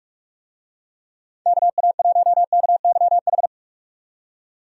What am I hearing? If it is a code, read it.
Morse code, heard as a keyed tone, about 46 wpm, 700 Hz. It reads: KA1KYH
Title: USA Call Signs - 6 Characters - 45wpm